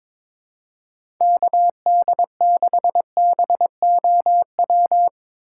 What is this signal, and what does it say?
Morse code, heard as a keyed tone, about 22 wpm, 700 Hz. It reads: KD6BOW
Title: USA Call Signs - 2x Character Spacing - 22wpm